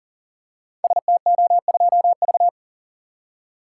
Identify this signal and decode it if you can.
Morse code, heard as a keyed tone, about 42 wpm, 700 Hz. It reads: STO2V
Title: Code Groups - 5 Letter:Numbers - 40wpm